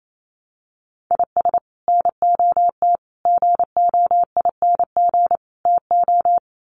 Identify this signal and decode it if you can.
Morse code, heard as a keyed tone, about 28 wpm, 700 Hz. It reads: IS NOT GOING TO
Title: Sets of 4 Words 28wpm